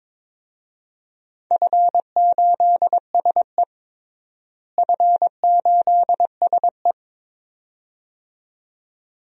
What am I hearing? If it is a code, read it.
Morse code, heard as a keyed tone, about 22 wpm, 700 Hz. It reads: F8SE F8SE
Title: DXCC-100 Call Signs - 4 Characters - Repeated 2x - 22wpm